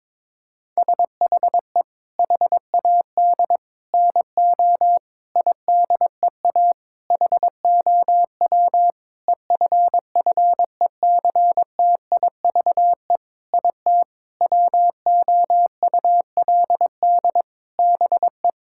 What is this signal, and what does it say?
Morse code, heard as a keyed tone, about 22 wpm, 700 Hz. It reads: SHE HAD NO IDEA HOW EFFECTIVE IT WOULD BE
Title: Sentences from Top 1000 Words - Review Part 6 - 22wpm